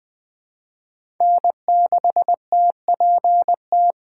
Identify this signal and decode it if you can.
Morse code, heard as a keyed tone, about 20 wpm, 700 Hz. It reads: N6TPT
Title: USA Call Signs - 5 Characters - 3x Character Spacing - 20wpm